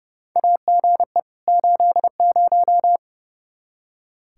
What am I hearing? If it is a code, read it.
Morse code, heard as a keyed tone, about 30 wpm, 700 Hz. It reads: AGE 80
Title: K1USN SST QSO Practice 30wpm